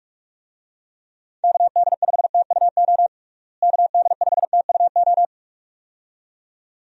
Morse code, heard as a keyed tone, about 46 wpm, 700 Hz. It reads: KD5TUO KD5TUO
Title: USA Call Signs - 6 Characters - Repeated 2x - 45wpm